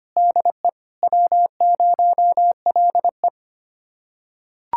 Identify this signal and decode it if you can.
Morse code, heard as a keyed tone, about 25 wpm, 700 Hz. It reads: DE W0LE
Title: USA Callsign DE Callsign 25wpm